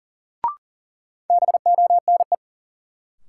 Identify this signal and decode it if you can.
Morse code, heard as a keyed tone, about 40 wpm, 700 Hz. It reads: BONE
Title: Sentences from Top 1600 Words - New Word Drill Alt - 40wpm